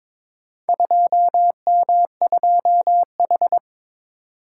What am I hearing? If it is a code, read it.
Morse code, heard as a keyed tone, about 22 wpm, 700 Hz. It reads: 2M2H
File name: practice / 22wpm / Four Letter-Number Combinations - 22wpm